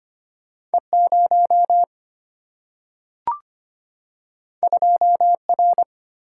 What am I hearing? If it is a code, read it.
Morse code, heard as a keyed tone, about 25 wpm, 700 Hz. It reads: E0 2R
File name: Two Letter-Number Combinations - 25wpm